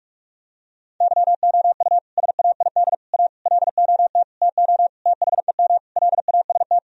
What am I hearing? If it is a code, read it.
Morse code, heard as a keyed tone, about 46 wpm, 700 Hz. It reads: YOU SAID A LOT TO THEM LAST
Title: Sentences from Top 1000 Words - Review Part 3 - 45wpm